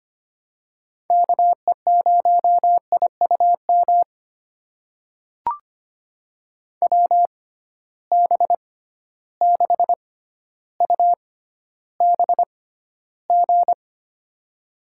Morse code, heard as a keyed tone, about 25 wpm, spaced as for 13 wpm, 700 Hz. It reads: KE0IUM WB6UBG
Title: USA Call Signs - 6 Characters - 6x Character Spacing - 25wpm